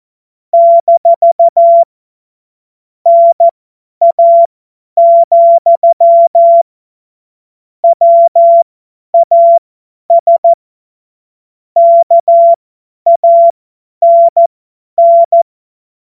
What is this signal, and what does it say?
Morse code, heard as a keyed tone, about 14 wpm, spaced as for 10 wpm, 680 Hz. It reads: - NA, WAS KANN
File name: Morsekurs / Witze / Witz 20. Dezember 2022 50 F70